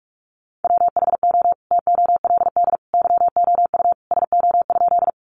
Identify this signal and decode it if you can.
Morse code, heard as a keyed tone, about 44 wpm, 700 Hz. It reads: WHO TOLD YOU SO?
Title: Sentences from Top 400 Words 45wpm